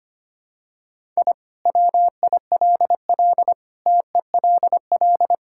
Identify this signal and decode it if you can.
Morse code, heard as a keyed tone, about 25 wpm, 690 Hz. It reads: I WILL TELL
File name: practice / 25wpm / Sentences with 2-Word Phrases - 25wpm